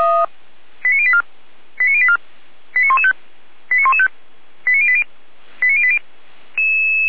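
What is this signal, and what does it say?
Five-tone selective call (ZVEI), heard as quick runs of five tones, tones of about 80 ms. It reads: 89084 89084 89115 89115 89080 89080
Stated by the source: ZVEI five-tone selective call 89092 received by SDR.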